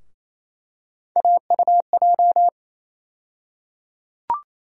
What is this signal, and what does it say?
Morse code, heard as a keyed tone, about 28 wpm, 700 Hz. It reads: AUJ
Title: USA Call Sign Postfixes - 3 Letters 28wpm